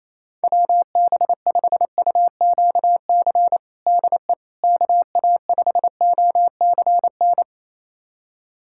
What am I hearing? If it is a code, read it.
Morse code, heard as a keyed tone, about 28 wpm, 700 Hz. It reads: WB5UQC DE KA5OCN
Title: USA Callsign DE Callsign - 6 Characters - 28wpm